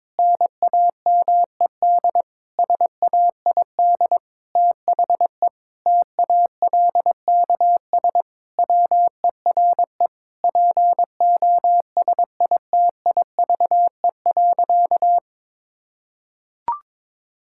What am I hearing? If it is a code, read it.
Morse code, heard as a keyed tone, about 22 wpm, 700 Hz. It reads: NAMED SAID THE TALKS WERE POSITIVE.
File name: practice / 22wpm / Sentences with 8-Word Phrases - 22wpm